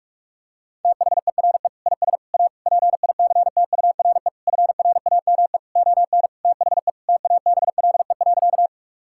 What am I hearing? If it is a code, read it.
Morse code, heard as a keyed tone, about 46 wpm, 700 Hz. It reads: THERE IS A PICTURE FRAME ON THE TABLE.
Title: Sentences from Top 1000 Words - Review Part 6 - 45wpm